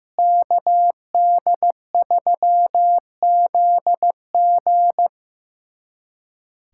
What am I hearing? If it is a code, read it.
Morse code, heard as a keyed tone, about 15 wpm, 700 Hz. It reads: KD3ZG
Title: USA Callsign DE Callsign - 5 Characters - 15wpm